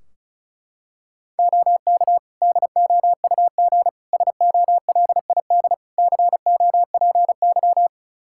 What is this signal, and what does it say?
Morse code, heard as a keyed tone, about 36 wpm, 700 Hz. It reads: OK DOUG SOLID COPY
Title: K1USN SST QSO Practice 35wpm